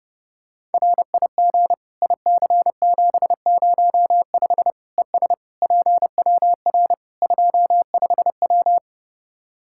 Morse code, heard as a keyed tone, about 30 wpm, 700 Hz. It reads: RIG IC705 ES PWR 25W
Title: Practice QSO - Exchange 2 - Info - 30wpm